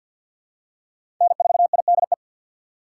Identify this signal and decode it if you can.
Morse code, heard as a keyed tone, about 47 wpm, 700 Hz. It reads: N4IDE
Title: USA Call Signs - 5 Characters - 6x Character Spacing - 50wpm